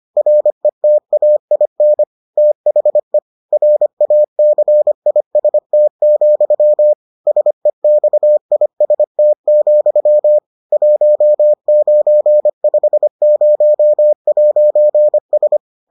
Morse code, heard as a keyed tone, about 25 wpm, 600 Hz. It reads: RETAIN THE RACIST, SEXIST, 1950'S